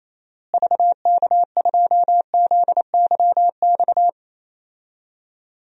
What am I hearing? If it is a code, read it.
Morse code, heard as a keyed tone, about 28 wpm, 700 Hz. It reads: VK2ZYX
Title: DXCC-100 Call Signs - 6 Characters - Repeated 3x - 28wpm